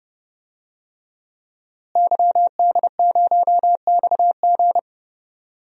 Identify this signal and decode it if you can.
Morse code, heard as a keyed tone, about 32 wpm, 700 Hz. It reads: YD0XG
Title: DXCC-100 Call Signs - 5 Characters - Repeated 2x - 30wpm